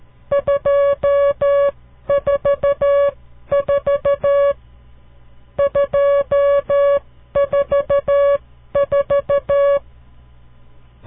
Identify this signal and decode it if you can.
Morse code, heard as a keyed tone, about 13 wpm, spaced as for 11 wpm, 560 Hz. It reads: 244 244
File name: CW Learning Programs / farnsworth / unit 5